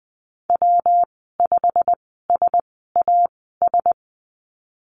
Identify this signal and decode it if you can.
Morse code, heard as a keyed tone, about 20 wpm, spaced as for 15 wpm, 700 Hz. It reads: W5SAS
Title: 173 Lesson 28g - Number 9 - Call Signs 20wpm